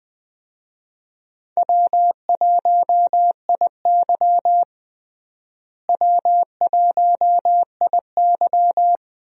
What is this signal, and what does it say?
Morse code, heard as a keyed tone, about 20 wpm, 700 Hz. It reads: W1IY W1IY
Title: USA Call Signs - 4 Characters - Repeated 2x - 20wpm